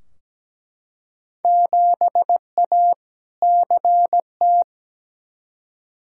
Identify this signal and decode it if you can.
Morse code, heard as a keyed tone, about 17 wpm, 700 Hz. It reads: 7A CT
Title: Field Day Exchanges - 17wpm